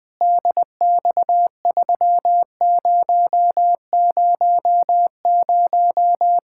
Morse code, heard as a keyed tone, about 20 wpm, 700 Hz. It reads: DX3000
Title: Practice QSO - Exchange 2 - Info - 20wpm